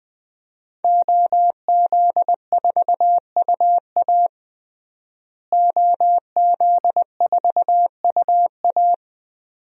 Morse code, heard as a keyed tone, about 20 wpm, 700 Hz. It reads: OZ4UA OZ4UA
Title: DXCC-100 Call Signs - 5 Characters - Repeated 2x - 20wpm